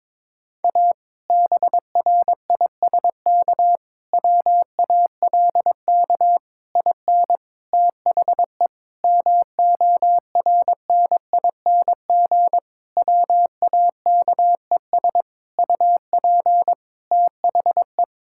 Morse code, heard as a keyed tone, about 22 wpm, 700 Hz. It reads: A BRISK WALK IN THE MORNING WAKES UP THE
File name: Difficult Sentences with Hints - 22wpm